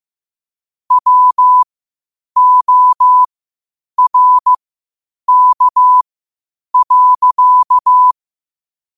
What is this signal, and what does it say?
Morse code, heard as a keyed tone, about 15 wpm, spaced as for 8 wpm, 1000 Hz. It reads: WORK.
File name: qotdfiles / 8wpm